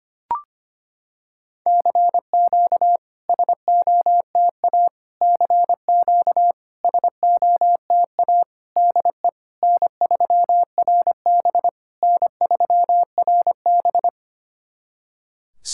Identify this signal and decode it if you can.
Morse code, heard as a keyed tone, about 25 wpm, 700 Hz. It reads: CQ SOTA CQ SOTA DE N3RB N3RB
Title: SOTA QSO Practice 25wpm